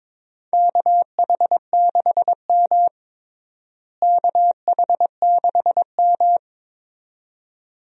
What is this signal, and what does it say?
Morse code, heard as a keyed tone, about 22 wpm, 700 Hz. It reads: KH6M KH6M
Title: DXCC-100 Call Signs - 4 Characters - Repeated 2x - 22wpm